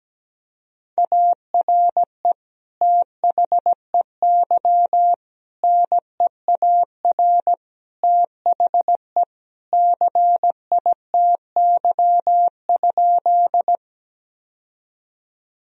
Morse code, heard as a keyed tone, about 17 wpm, 700 Hz. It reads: ARE THEY NEAR THE CITY?
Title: Sentences from Top 1000 Words - Review Part 1 - 17wpm